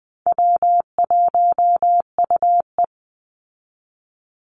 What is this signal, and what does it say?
Morse code, heard as a keyed tone, about 20 wpm, 700 Hz. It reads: W1UE
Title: Easy Call Signs Repeated 3x - 20wpm